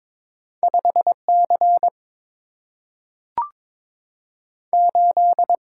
Morse code, heard as a keyed tone, about 22 wpm, 700 Hz. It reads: 5 C 8
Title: Single-Letter Single-Number 22wpm